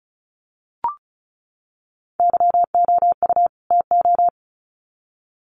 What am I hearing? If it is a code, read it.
Morse code, heard as a keyed tone, about 34 wpm, 700 Hz. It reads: YOU TO
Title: Sets of 2 Words 35wpm